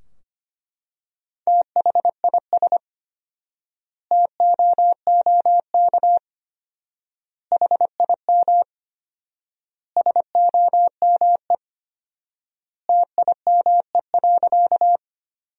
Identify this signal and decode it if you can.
Morse code, heard as a keyed tone, about 25 wpm, 700 Hz. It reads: THIS TOOK HIM SOME TIME.
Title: 4x Word-Spacing - Sentences from Top 300 Words - 25wpm